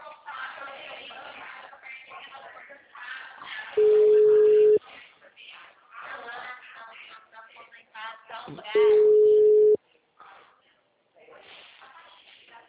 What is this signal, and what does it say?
Ringback tone (Europe, 425 Hz), with speech.
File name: out- -2010- - - .5858